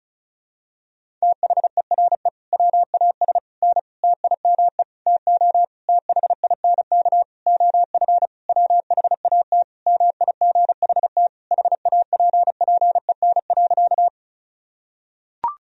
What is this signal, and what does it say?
Morse code, heard as a keyed tone, about 36 wpm, 700 Hz. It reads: THERE WAS N TIME TO THINK OF WHAT MIGHT HAPPEN.
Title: Sentences from Top 400 Words and Abbreviations 35wpm